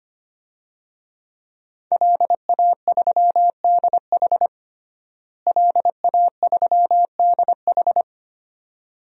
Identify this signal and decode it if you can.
Morse code, heard as a keyed tone, about 25 wpm, 700 Hz. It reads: LA3DH LA3DH
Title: DXCC-100 Call Signs - 5 Characters - Repeated 2x - 25wpm